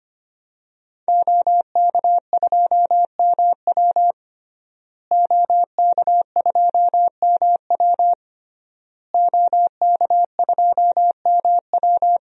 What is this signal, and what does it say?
Morse code, heard as a keyed tone, about 25 wpm, 700 Hz. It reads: OK2MW OK2MW OK2MW
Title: DXCC-100 Call Signs - 5 Characters - Repeated 3x - 25wpm